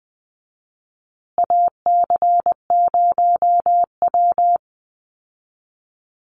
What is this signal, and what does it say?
Morse code, heard as a keyed tone, about 20 wpm, 700 Hz. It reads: AC0W
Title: Call Signs Repeated 2x - 20wpm